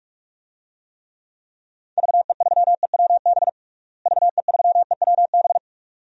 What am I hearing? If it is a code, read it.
Morse code, heard as a keyed tone, about 46 wpm, 700 Hz. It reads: VE3EWB VE3EWB
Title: DXCC-100 Call Signs - 6 Characters - Repeated 2x - 45wpm